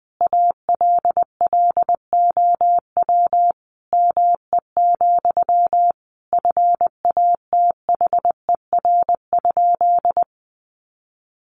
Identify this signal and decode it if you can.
Morse code, heard as a keyed tone, about 20 wpm, 700 Hz. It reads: ALLOW ME, FATHER?
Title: Sentences from Top 700 Words 20wpm